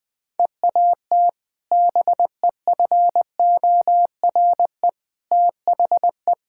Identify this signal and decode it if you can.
Morse code, heard as a keyed tone, about 20 wpm, 700 Hz. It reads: EAT BEFORE THE
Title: Sentences with 2-Word Phrases - 20wpm